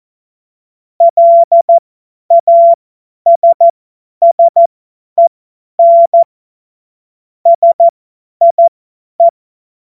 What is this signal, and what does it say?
Morse code, heard as a keyed tone, about 14 wpm, spaced as for 10 wpm, 680 Hz. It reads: LASSEN SIE
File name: Morsekurs / Witze / Witz 27. Juli 2021, 50 F70